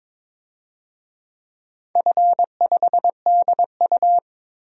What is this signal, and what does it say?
Morse code, heard as a keyed tone, about 23 wpm, 700 Hz. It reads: F5DU
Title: DXCC-100 Call Signs - 4 Characters - 22wpm